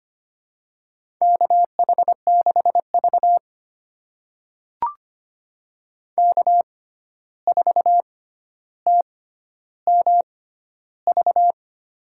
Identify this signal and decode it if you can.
Morse code, heard as a keyed tone, about 25 wpm, spaced as for 12 wpm, 700 Hz. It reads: KH6V K4TMV
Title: USA Call Signs - 6x Character Spacing - 25wpm